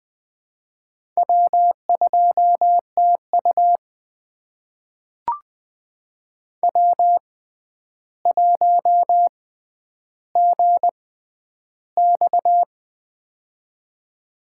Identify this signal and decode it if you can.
Morse code, heard as a keyed tone, about 20 wpm, spaced as for 10 wpm, 700 Hz. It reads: W2TU W1GX
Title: USA Call Signs - 4 Characters - 6x Character Spacing - 20wpm